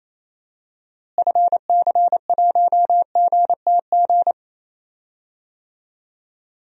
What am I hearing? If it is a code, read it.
Morse code, heard as a keyed tone, about 27 wpm, 700 Hz. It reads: FC1GTG
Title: DXCC-100 Call Signs - 6 Characters - Repeated 3x - 28wpm